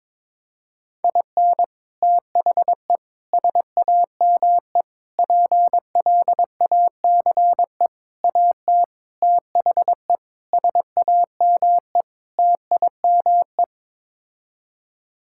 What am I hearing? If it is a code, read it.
Morse code, heard as a keyed tone, about 22 wpm, 700 Hz. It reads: IN THE SAME PLACE AT THE SAME TIME
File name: practice / 22wpm / Sets of 8 Words - 22wpm